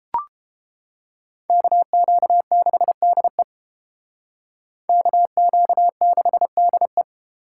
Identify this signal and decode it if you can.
Morse code, heard as a keyed tone, about 30 wpm, 700 Hz. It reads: KQ6DE KQ6DE